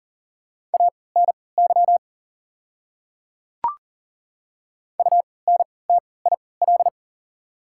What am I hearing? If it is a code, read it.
Morse code, heard as a keyed tone, about 40 wpm, spaced as for 23 wpm, 690 Hz. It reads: ANY UNTIL
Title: 3x Character Spacing - Top 300 Words - 40wpm